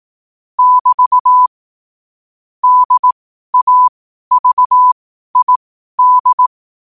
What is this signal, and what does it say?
Morse code, heard as a keyed tone, about 18 wpm, spaced as for 12 wpm, 1000 Hz. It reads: <BT> DAVID
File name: qotdfiles / 12wpm